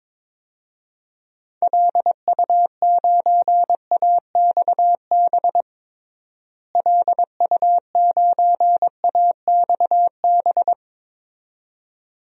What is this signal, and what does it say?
Morse code, heard as a keyed tone, about 22 wpm, 700 Hz. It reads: LU9AXB LU9AXB
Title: DXCC-100 Call Signs - 6 Characters - Repeated 2x - 22wpm